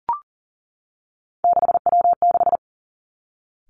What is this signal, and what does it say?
Morse code, heard as a keyed tone, about 41 wpm, 700 Hz. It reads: 6W6
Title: Three Challenging Letter-Number Combinations 40wpm